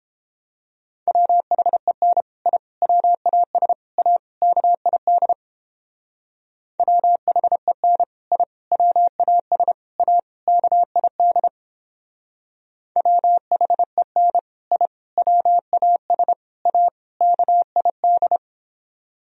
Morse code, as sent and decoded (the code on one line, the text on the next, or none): .-- .... -. / .. / .-- .- ... / .- / -.- .. -.. / .-- .... -. / .. / .-- .- ... / .- / -.- .. -.. / .-- .... -. / .. / .-- .- ... / .- / -.- .. -..
WHEN I WAS A KID WHEN I WAS A KID WHEN I WAS A KID